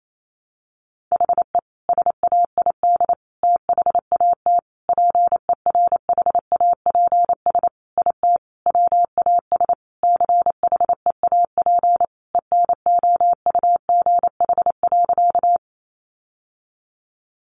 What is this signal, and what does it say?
Morse code, heard as a keyed tone, about 28 wpm, 700 Hz. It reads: HE SAID THAT PERHAPS IT WAS CHEAP ENOUGH.
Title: Sentences from Top 1200 Words 28wpm